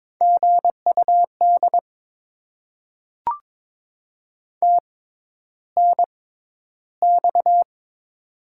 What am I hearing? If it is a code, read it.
Morse code, heard as a keyed tone, about 22 wpm, spaced as for 11 wpm, 690 Hz. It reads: GUD TNX
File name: Top 20 QSO Elements - 6x Character Spacing - 22wpm